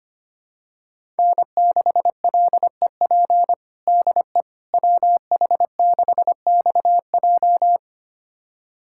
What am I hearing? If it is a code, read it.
Morse code, heard as a keyed tone, about 25 wpm, 700 Hz. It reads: N6LEP DE WH6XJ
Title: USA Callsign DE Callsign - 5 Characters - 25wpm